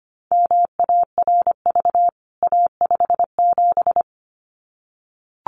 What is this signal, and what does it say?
Morse code, heard as a keyed tone, about 25 wpm, 700 Hz. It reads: MARV A57
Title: CWT Exchange 25wpm